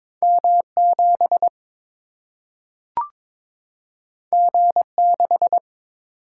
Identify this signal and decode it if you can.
Morse code, heard as a keyed tone, about 22 wpm, 700 Hz. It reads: M 7 G 6
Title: Single-Letter Single-Number 22wpm